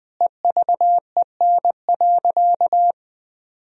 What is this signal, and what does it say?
Morse code, heard as a keyed tone, about 20 wpm, 700 Hz. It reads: EVEN.
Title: Sentences from Top 1000 Words - Review Part 1 - 20wpm